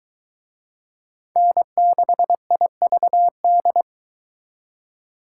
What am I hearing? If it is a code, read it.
Morse code, heard as a keyed tone, about 24 wpm, 700 Hz. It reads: N6IVD